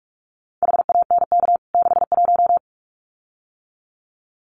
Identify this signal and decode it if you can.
Morse code, heard as a keyed tone, about 44 wpm, 700 Hz. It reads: HANK 61
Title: CWT Exchange 45wpm